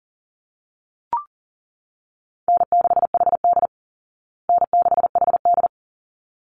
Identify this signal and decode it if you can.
Morse code, heard as a keyed tone, about 40 wpm, 700 Hz. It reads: N6HD N6HD
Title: Call Signs Repeated 3x - 40wpm